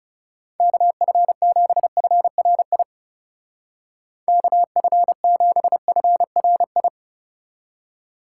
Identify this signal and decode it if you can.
Morse code, heard as a keyed tone, about 40 wpm, 700 Hz. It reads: KF7FRI KF7FRI